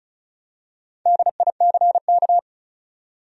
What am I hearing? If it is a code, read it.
Morse code, heard as a keyed tone, about 35 wpm, 700 Hz. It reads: DICK
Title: USA Common Names - Max 5 Letters - 35wpm